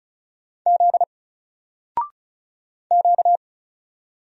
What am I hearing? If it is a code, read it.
Morse code, heard as a keyed tone, about 36 wpm, 700 Hz. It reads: Z Q
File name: Single-Letter-Number-35wpm